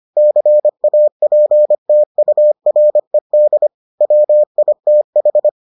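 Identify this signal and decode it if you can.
Morse code, heard as a keyed tone, about 25 wpm, 600 Hz. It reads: CAPTURED WITH
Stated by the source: Morse edition of the Foundations of Amateur Radio episodes.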